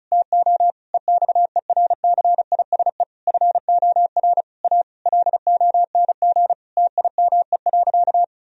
Morse code, heard as a keyed tone, about 36 wpm, 700 Hz. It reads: TO EXERCISE FOR A LONG TIME.
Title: Sentences from Top 1000 Words - Review Part 4 - 35wpm